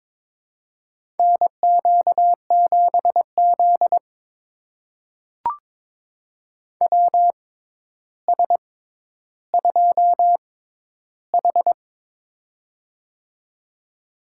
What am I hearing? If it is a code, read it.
Morse code, heard as a keyed tone, about 22 wpm, spaced as for 11 wpm, 700 Hz. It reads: NQ7Z WS2H
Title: USA Call Signs - 6x Character Spacing - 22wpm